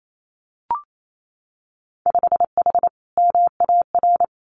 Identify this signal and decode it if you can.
Morse code, heard as a keyed tone, about 28 wpm, 700 Hz. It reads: 5H MAR
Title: Winter Field Day Exchanges Encore 28wpm